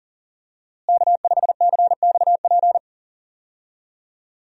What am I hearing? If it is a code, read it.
Morse code, heard as a keyed tone, about 40 wpm, 700 Hz. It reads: K5CXP
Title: USA Call Signs - 5 Characters - Repeated 3x - 40wpm